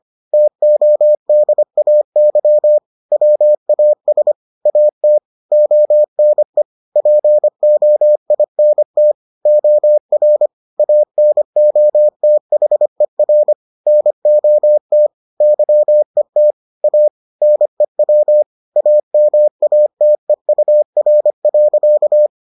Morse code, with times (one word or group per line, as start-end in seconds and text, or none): 0.3-2.8 TODAY
3.1-4.3 WAS
4.7-5.2 AT
5.5-6.6 ONE
7.0-9.1 POINT
9.5-10.5 OR
10.8-13.5 ANOTHER
13.9-15.1 NOT
15.4-16.5 YET
16.8-17.1 A
17.4-18.4 NEW
18.8-22.3 AMATEUR.